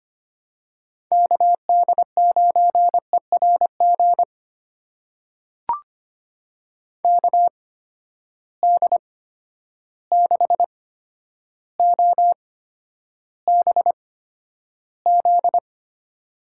Morse code, heard as a keyed tone, about 25 wpm, spaced as for 11 wpm, 700 Hz. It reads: KD9ERG KD6OBZ
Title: USA Call Signs - 6 Characters - 8x Character Spacing - 25wpm